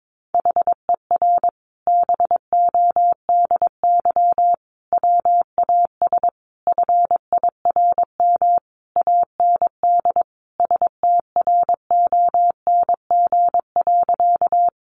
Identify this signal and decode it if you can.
Morse code, heard as a keyed tone, about 22 wpm, 700 Hz. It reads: HER BODY WAS FIRM AND STRONG.
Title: Sentences from Top 1400 Words 22wpm